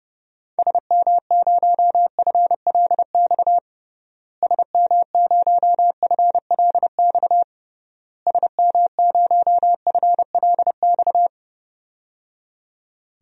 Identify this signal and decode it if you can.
Morse code, heard as a keyed tone, about 32 wpm, 700 Hz. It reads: SM0FLX SM0FLX SM0FLX
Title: DXCC-100 Call Signs - 6 Characters - Repeated 3x - 30wpm